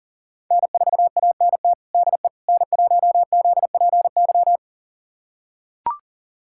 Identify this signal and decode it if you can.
Morse code, heard as a keyed tone, about 37 wpm, 700 Hz. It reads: N4ANT DE N1ZPY
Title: USA Callsign DE Callsign - 5 Characters - 40wpm